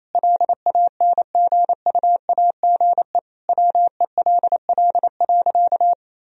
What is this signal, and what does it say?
Morse code, heard as a keyed tone, about 28 wpm, 700 Hz. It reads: LANGUAGE WELL.
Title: Sentences from Top 1000 Words - Review Part 3 - 28wpm